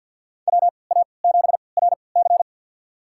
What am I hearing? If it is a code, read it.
Morse code, heard as a keyed tone, about 51 wpm, spaced as for 28 wpm, 700 Hz. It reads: WA7RC
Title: USA Call Signs - 5 Characters - 3x Character Spacing - 50wpm